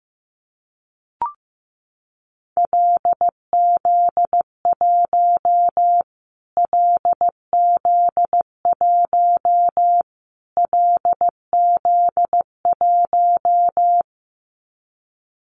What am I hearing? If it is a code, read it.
Morse code, heard as a keyed tone, about 15 wpm, 700 Hz. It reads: LZ1 LZ1 LZ1
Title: Callsign Prefixes for Top 25 DXCC - Repeated 3x - 15wpm